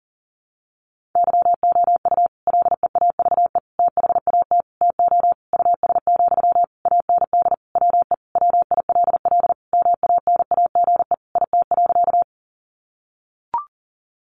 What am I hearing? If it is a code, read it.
Morse code, heard as a keyed tone, about 40 wpm, 700 Hz. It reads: YOU LEAVE THAT TO US, AND WE WILL MANAGE IT.
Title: Sentences from Top 1400 Words 40wpm